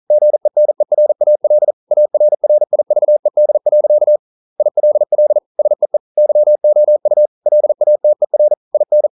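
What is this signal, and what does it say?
Morse code, heard as a keyed tone, about 43 wpm, 600 Hz. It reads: GENERAL ARRIVED. ILL SEE YOU LATER IN
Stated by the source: Morse code at 40 wpm, practice text, CW med Gnister.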